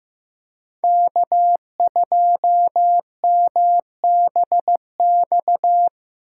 Morse code, heard as a keyed tone, about 15 wpm, 700 Hz. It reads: K2MBX
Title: Speed Racing - USA 1x3 Call Signs 15wpm